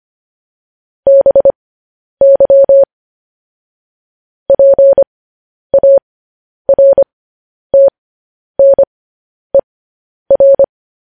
Morse code, heard as a keyed tone, about 25 wpm, spaced as for 10 wpm, 550 Hz. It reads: BY PARTNER
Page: News Headlines in Morse Code at 10 WPM